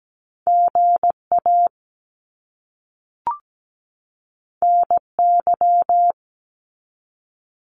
Morse code, heard as a keyed tone, about 17 wpm, 700 Hz. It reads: GA NY
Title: US States 17wpm